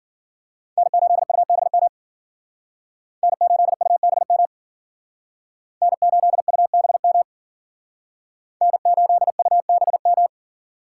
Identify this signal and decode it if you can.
Morse code, heard as a keyed tone, about 51 wpm, 700 Hz. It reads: N8UBM N8UBM N8UBM N8UBM